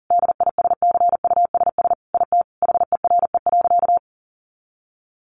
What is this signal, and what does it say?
Morse code, heard as a keyed tone, about 40 wpm, 700 Hz. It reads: DISCUSS IT HERE.
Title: Sentences from Top 1100 Words 40wpm